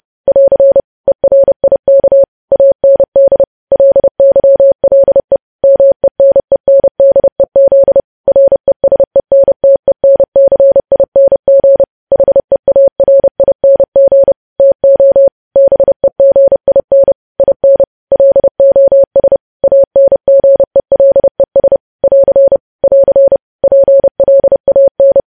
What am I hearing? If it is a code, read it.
Morse code, heard as a keyed tone, about 30 wpm, 550 Hz. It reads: <AR> ERIK AND LYLE MENENDEZ RESENTENCING HEARING TO BEGIN IN LOS ANGELES <AR> <AR> PLAN
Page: news headlines in morse code at 30 wpm